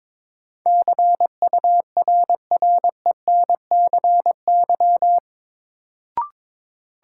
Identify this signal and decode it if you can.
Morse code, heard as a keyed tone, about 22 wpm, 700 Hz. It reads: CURRENCY
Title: Sentences from Top 1500 Words - New Word Drill - 22wpm